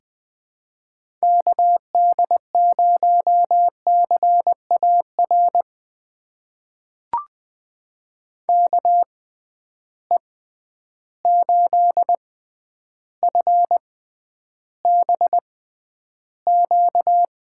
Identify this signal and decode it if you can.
Morse code, heard as a keyed tone, about 20 wpm, spaced as for 10 wpm, 700 Hz. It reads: KD0CAR KE8FBQ
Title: USA Call Signs - 6 Characters - 6x Character Spacing - 20wpm